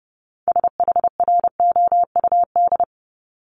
Morse code, heard as a keyed tone, about 30 wpm, 700 Hz. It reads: SHROUD
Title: Multi-letter Phonemes 30wpm